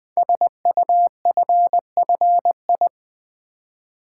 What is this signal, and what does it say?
Morse code, heard as a keyed tone, about 20 wpm, 700 Hz. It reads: SUFFI
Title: Build A Word - 5 Letters At A Time - 12-Letter Words - 20wpm